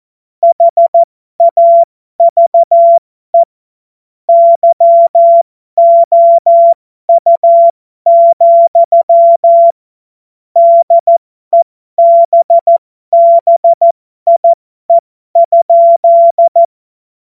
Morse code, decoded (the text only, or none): HAVE YOU, DEBBIE?